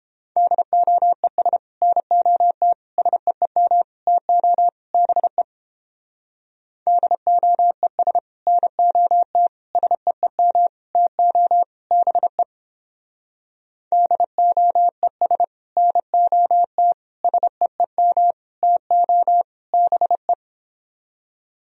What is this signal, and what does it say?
Morse code, heard as a keyed tone, about 29 wpm, 700 Hz. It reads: DOES NOT SEEM TO BE DOES NOT SEEM TO BE DOES NOT SEEM TO BE